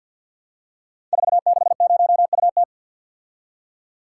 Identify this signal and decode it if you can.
Morse code, heard as a keyed tone, about 47 wpm, 700 Hz. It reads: 460UT
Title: Code Groups - 5 Letter:Numbers - 50wpm